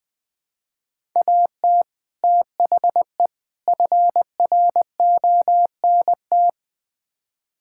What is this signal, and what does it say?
Morse code, heard as a keyed tone, about 20 wpm, 700 Hz. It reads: AT THE FRONT
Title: Build A Phrase - 5-Word Phrases - 20wpm